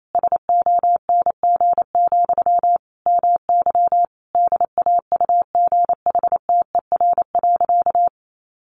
Morse code, heard as a keyed tone, about 28 wpm, 700 Hz. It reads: SONG, MY DAUGHTER.
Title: Sentences from Top 1500 Words 28wpm